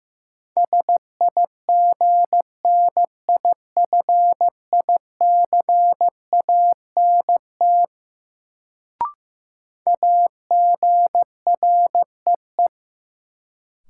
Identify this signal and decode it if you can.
Morse code, heard as a keyed tone, about 15 wpm, 700 Hz. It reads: SIGNIFICANT AGREE
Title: Sentences from Top 1000 Words - Review Part 6 - New Word Drill - 15wpm